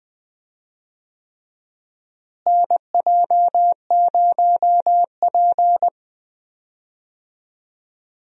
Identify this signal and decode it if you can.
Morse code, heard as a keyed tone, about 20 wpm, 700 Hz. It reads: NJ0P
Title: USA Call Signs - 4 Characters - Repeated 3x - 20wpm